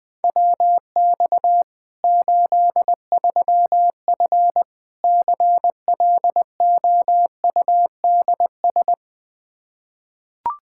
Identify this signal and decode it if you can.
Morse code, heard as a keyed tone, about 20 wpm, 700 Hz. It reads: WX 83F CLOUDS
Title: Practice QSO - Exchange 2 - Info - 20wpm